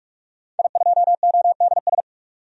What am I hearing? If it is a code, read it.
Morse code, heard as a keyed tone, about 41 wpm, 700 Hz. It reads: I2ODS
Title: DXCC-100 Call Signs - 5 Characters - 45wpm